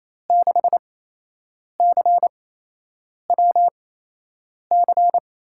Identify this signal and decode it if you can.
Morse code, heard as a keyed tone, about 28 wpm, spaced as for 8 wpm, 700 Hz. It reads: 6CWC
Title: USA Call Signs - 6 Characters - 8x Character Spacing - 28wpm